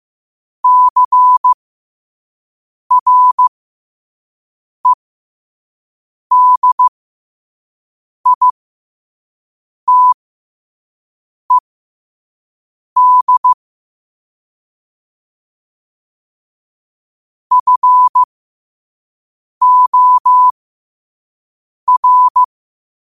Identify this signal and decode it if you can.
Morse code, heard as a keyed tone, about 15 wpm, spaced as for 5 wpm, 1000 Hz. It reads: CREDITED FOR
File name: qotdfiles / 5wpm